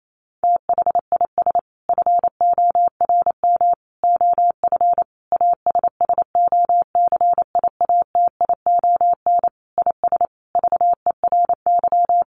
Morse code, heard as a keyed tone, about 28 wpm, 700 Hz. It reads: THIS FORM OF ASSOCIATION IS VERY
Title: Sentences from Top 1100 Words 28wpm